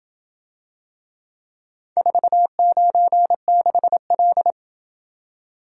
Morse code, heard as a keyed tone, about 27 wpm, 700 Hz. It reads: 496L